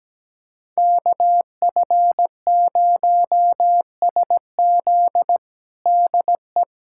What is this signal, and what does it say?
Morse code, heard as a keyed tone, about 17 wpm, 700 Hz. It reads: KF0SZ DE
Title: USA Callsign DE Callsign - 5 Characters - 17wpm